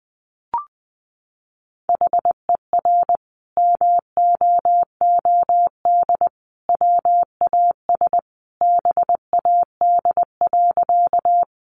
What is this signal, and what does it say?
Morse code, heard as a keyed tone, about 20 wpm, 700 Hz. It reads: HER MOOD WAS BAD.
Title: Sentences from Top 1100 Words 20wpm